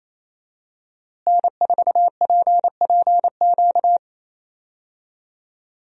Morse code, heard as a keyed tone, about 27 wpm, 700 Hz. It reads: N4PPQ
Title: USA Call Signs - 5 Characters - Repeated 3x - 28wpm